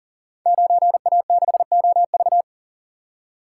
Morse code, heard as a keyed tone, about 40 wpm, 690 Hz. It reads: 9A6OV
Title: Code Groups - 5 Letter:Numbers - 40wpm